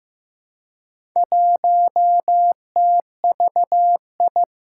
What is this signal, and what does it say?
Morse code, heard as a keyed tone, about 15 wpm, 700 Hz. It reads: 1TVI
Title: Four Challenging Letter-Number Combinations 15wpm